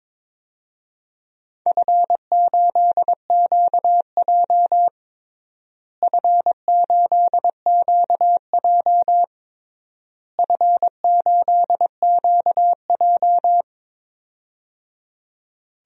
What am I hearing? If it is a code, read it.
Morse code, heard as a keyed tone, about 22 wpm, 700 Hz. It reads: F8QJ F8QJ F8QJ
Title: DXCC-100 Call Signs - 4 Characters - Repeated 3x - 22wpm